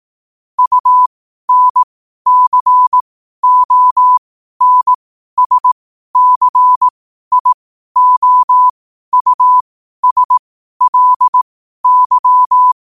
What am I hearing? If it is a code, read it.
Morse code, heard as a keyed tone, about 18 wpm, spaced as for 12 wpm, 1000 Hz. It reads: UNCONSCIOUSLY
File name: qotdfiles / 12wpm